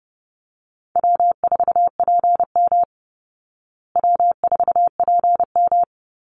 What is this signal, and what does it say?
Morse code, heard as a keyed tone, about 30 wpm, 700 Hz. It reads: W4PM W4PM
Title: Easy Call Signs Repeated 2x - 30wpm